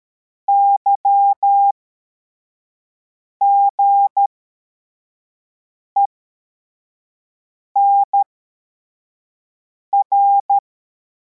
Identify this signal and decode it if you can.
Morse code, heard as a keyed tone, about 13 wpm, spaced as for 4 wpm, 800 Hz. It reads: YGENR
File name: formation / cw / Text55